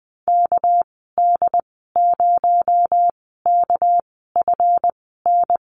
Morse code, heard as a keyed tone, about 20 wpm, spaced as for 14 wpm, 700 Hz. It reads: KD0KFN
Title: 231 Lesson 37g - Letter x - Call Signs 20wpm